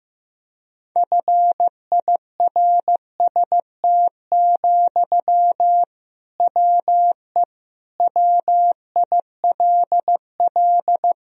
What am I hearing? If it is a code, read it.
Morse code, heard as a keyed tone, about 15 wpm, 700 Hz. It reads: FIRST, WE WILL
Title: Sentences from Top 1000 Words - Review Part 1 - 15wpm